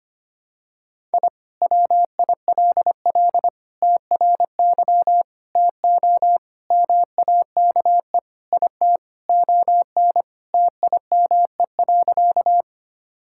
Morse code, heard as a keyed tone, about 25 wpm, 700 Hz. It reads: I WILL TRY TO MAKE IT ON TIME.
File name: practice / 25wpm / Sentences with 3-Word Phrases - 25wpm